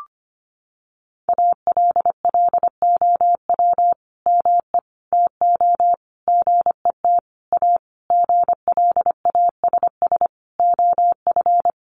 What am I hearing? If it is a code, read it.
Morse code, heard as a keyed tone, about 25 wpm, 700 Hz. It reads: ALLOW ME TO GET A GLASS OF
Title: Sentences from Top 1300 Words 25wpm